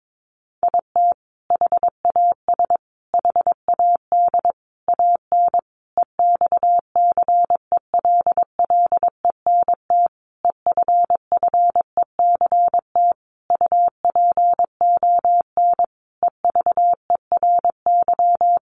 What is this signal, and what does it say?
Morse code, heard as a keyed tone, about 22 wpm, 700 Hz. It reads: IT HAS HAD AN EXCELLENT EFFECT UPON EVERY
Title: Sentences from Top 1100 Words 22wpm